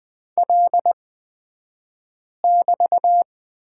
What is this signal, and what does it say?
Morse code, heard as a keyed tone, about 20 wpm, spaced as for 5 wpm, 700 Hz. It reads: L<BT>
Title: 223 Lesson 36d - BT - Two Characters 20wpm